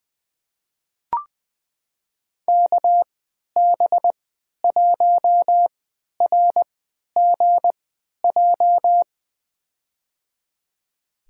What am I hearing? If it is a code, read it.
Morse code, heard as a keyed tone, about 20 wpm, spaced as for 11 wpm, 700 Hz. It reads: KB1RGJ
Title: USA Call Signs - 3x Character Spacing - 20wpm